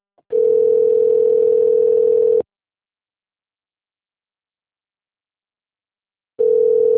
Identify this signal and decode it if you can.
Ringback tone (North America), 440 + 480 Hz.